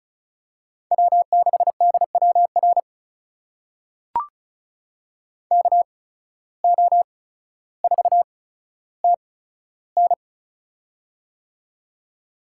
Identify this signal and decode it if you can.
Morse code, heard as a keyed tone, about 36 wpm, spaced as for 15 wpm, 700 Hz. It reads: W6DWR KO4TN
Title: USA Call Signs - 5 Characters - 8x Character Spacing - 35wpm